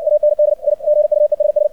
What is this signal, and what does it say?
Morse code, heard as a keyed tone, about 60 wpm, 600 Hz, with radio noise. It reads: AND I GUESS
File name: archived / CRSnet